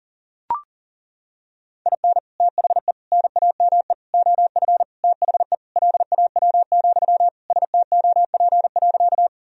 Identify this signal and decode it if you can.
Morse code, heard as a keyed tone, about 40 wpm, 700 Hz. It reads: IN THE NAME OF THE LAW, STOP.
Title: Sentences from Top 1000 Words - Review Part 3 - 40wpm